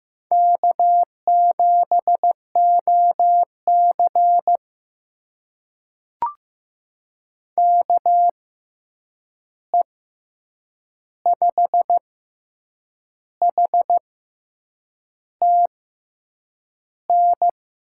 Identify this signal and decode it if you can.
Morse code, heard as a keyed tone, about 15 wpm, spaced as for 7 wpm, 700 Hz. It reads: K7OC KE5HTN
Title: USA Call Signs - 6x Character Spacing - 15wpm